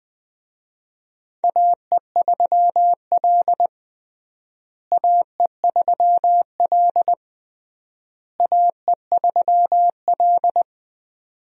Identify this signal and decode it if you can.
Morse code, heard as a keyed tone, about 19 wpm, 700 Hz. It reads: AE3L AE3L AE3L
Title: USA Call Signs - 4 Characters - Repeated 3x - 20wpm